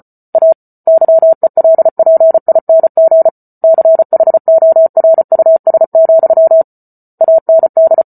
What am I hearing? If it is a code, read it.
Morse code, heard as a keyed tone, about 34 wpm, 650 Hz. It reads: A YELPING CHORUS, AND